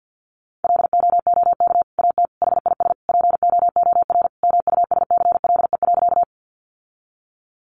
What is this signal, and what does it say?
Morse code, heard as a keyed tone, about 50 wpm, 700 Hz. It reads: LOOK AT HIS POOR MUSCLE.
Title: Sentences from Top 1300 Words 50wpm